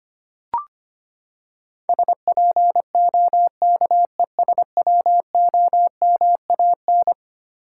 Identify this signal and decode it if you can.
Morse code, heard as a keyed tone, about 25 wpm, 700 Hz. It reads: SPOKESWOMAN
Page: morse-code-ninja-practice-25wpm